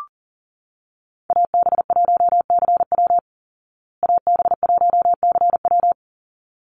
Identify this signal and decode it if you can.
Morse code, heard as a keyed tone, about 40 wpm, 700 Hz. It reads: AB1CW AB1CW
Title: Call Signs Repeated 3x - 40wpm